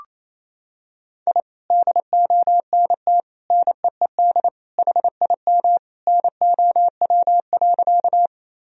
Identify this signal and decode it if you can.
Morse code, heard as a keyed tone, about 28 wpm, 700 Hz. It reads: I DONT NEED HIM NOW.
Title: Sentences from Top 1000 Words - Review Part 1 - 28wpm